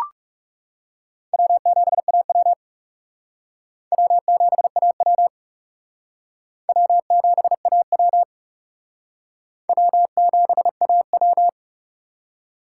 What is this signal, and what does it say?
Morse code, heard as a keyed tone, about 36 wpm, 700 Hz. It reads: W7AW W7AW W7AW W7AW
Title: Speed Racing - USA 1x2 Call Signs 30wpm